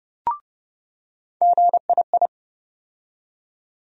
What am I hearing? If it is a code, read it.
Morse code, heard as a keyed tone, about 30 wpm, 700 Hz. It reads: GII
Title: USA Call Sign Postfixes - 3 Letters 30wpm